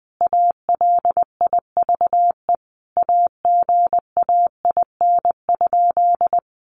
Morse code, heard as a keyed tone, about 20 wpm, 700 Hz. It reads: ALIVE AGAIN?
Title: A Sound Of Thunder 20wpm